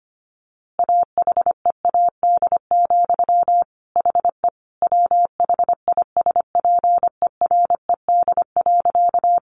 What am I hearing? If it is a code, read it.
Morse code, heard as a keyed tone, about 25 wpm, 700 Hz. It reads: AHEAD, HE WHISPERED.
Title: A Sound Of Thunder 25wpm